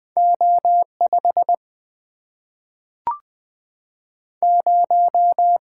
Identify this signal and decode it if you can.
Morse code, heard as a keyed tone, about 20 wpm, 700 Hz. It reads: O5 0
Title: Single-Letter Single-Number 20wpm